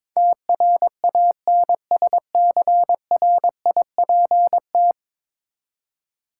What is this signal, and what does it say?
Morse code, heard as a keyed tone, about 22 wpm, 700 Hz. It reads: TRANSCRIPT
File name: practice / 22wpm / 10 Letter Words - First 3 Letters Given - 22wpm